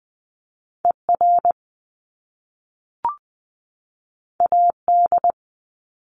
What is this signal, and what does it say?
Morse code, heard as a keyed tone, about 19 wpm, 700 Hz. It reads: E R A D
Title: Top Two Letter Combinations in Words 20wpm